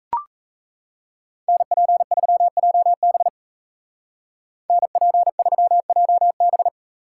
Morse code, heard as a keyed tone, about 34 wpm, 700 Hz. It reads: NP3JB NP3JB
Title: Speed Racing - USA Call Signs 28wpm